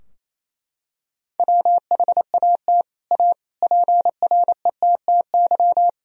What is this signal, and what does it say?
Morse code, heard as a keyed tone, about 28 wpm, 700 Hz. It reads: WHAT A PRETTY
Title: Sentences from Top 1000 Words - Review Part 3 - 28wpm